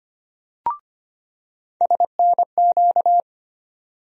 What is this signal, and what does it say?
Morse code, heard as a keyed tone, about 25 wpm, 700 Hz. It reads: SNQ
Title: USA Call Sign Postfixes - 3 Letters 25wpm